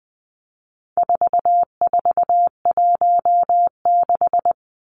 Morse code, heard as a keyed tone, about 20 wpm, 700 Hz. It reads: 4416
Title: Numbers - 4-Digit 20wpm